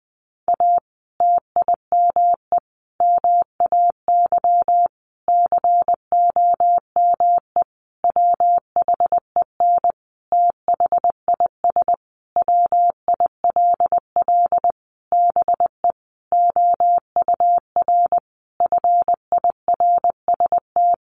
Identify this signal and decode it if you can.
Morse code, heard as a keyed tone, about 20 wpm, 700 Hz. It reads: A TIME MAY COME WHEN THIS WILL BE OUR FIRST
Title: Sentences from Top 1500 Words 20wpm